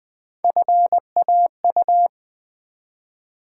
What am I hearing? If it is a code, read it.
Morse code, heard as a keyed tone, about 20 wpm, 700 Hz. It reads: FAU
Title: Build A Word - 1 Letter At A Time - 3 to 5-Letter Words - 20wpm